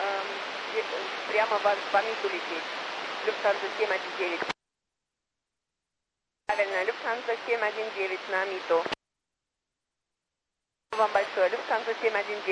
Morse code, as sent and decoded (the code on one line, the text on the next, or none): none